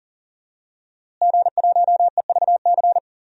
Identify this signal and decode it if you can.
Morse code, heard as a keyed tone, about 37 wpm, 700 Hz. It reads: G1EVC
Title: DXCC-100 Call Signs - 5 Characters - Repeated 3x - 40wpm